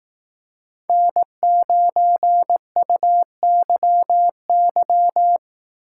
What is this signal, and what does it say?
Morse code, heard as a keyed tone, about 20 wpm, 700 Hz. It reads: N9UYY